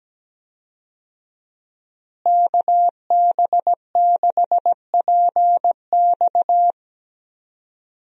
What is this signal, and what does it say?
Morse code, heard as a keyed tone, about 17 wpm, 700 Hz. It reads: KB6PX
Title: USA Call Signs - 5 Characters - Repeated 3x - 17wpm